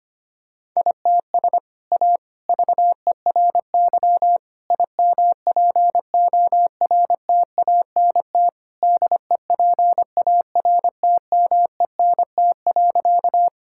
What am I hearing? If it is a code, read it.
Morse code, heard as a keyed tone, about 25 wpm, 700 Hz. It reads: ITS A VERY IMPORTANT DEPARTMENT.
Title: Sentences from Top 1000 Words - Review Part 5 - 25wpm